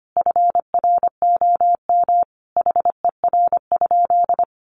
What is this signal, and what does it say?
Morse code, heard as a keyed tone, about 25 wpm, 700 Hz. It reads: FROM HER?
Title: Sentences from Top 1500 Words 25wpm